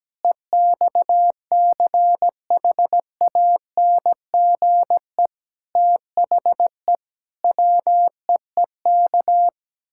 Morse code, heard as a keyed tone, about 17 wpm, 700 Hz. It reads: EXCHANGE THE WEEK
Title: Sentences from Top 1000 Words - Review Part 6 - 17wpm